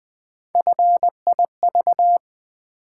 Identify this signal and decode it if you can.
Morse code, heard as a keyed tone, about 20 wpm, 700 Hz. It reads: FIV
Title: Build A Word - 1 Letter At A Time - 4-Letter Words - 20wpm